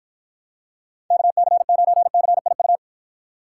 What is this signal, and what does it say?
Morse code, heard as a keyed tone, about 52 wpm, 700 Hz. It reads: KC9CIU